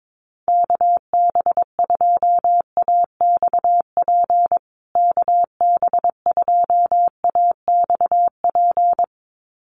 Morse code, heard as a keyed tone, about 22 wpm, 700 Hz. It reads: KB2AXP KB2AXP
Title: USA Call Signs Repeated 2x 22wpm